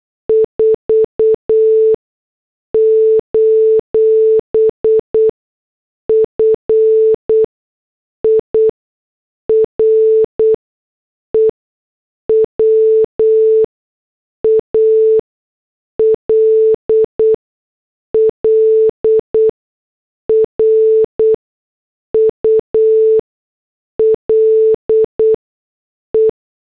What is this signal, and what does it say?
Morse code, heard as a keyed tone, about 8 wpm, spaced as for 6 wpm, 440 Hz. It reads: 4: FIREWALL RULE